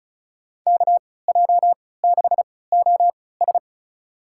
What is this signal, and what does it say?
Morse code, heard as a keyed tone, about 36 wpm, spaced as for 20 wpm, 700 Hz. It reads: KJ6OS
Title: USA Call Signs - 5 Characters - 3x Character Spacing - 35wpm